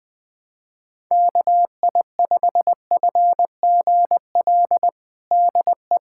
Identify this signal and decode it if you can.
Morse code, heard as a keyed tone, about 20 wpm, 700 Hz. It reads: KI5FGL DE
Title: USA Callsign DE Callsign - 6 Characters - 20wpm